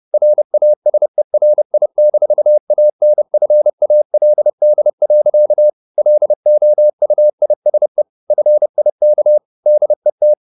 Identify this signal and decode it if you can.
Morse code, heard as a keyed tone, about 30 wpm, 600 Hz. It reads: RASERI-ANFALD. LOUISE FIK DET
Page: Ferie 30 wpm